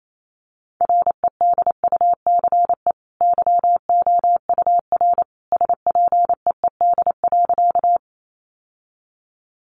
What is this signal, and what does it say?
Morse code, heard as a keyed tone, about 28 wpm, 700 Hz. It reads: REDUCE YOUR SPEED.
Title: Sentences from Top 1100 Words 28wpm